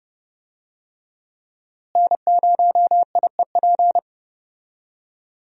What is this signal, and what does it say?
Morse code, heard as a keyed tone, about 30 wpm, 700 Hz. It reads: N0IEP
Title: USA Call Signs - 5 Characters - Repeated 3x - 30wpm